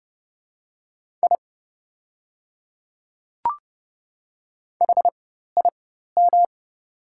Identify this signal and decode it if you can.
Morse code, heard as a keyed tone, about 30 wpm, spaced as for 14 wpm, 700 Hz. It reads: I HIM
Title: 4x Character Spacing - Top 100 Words - 30wpm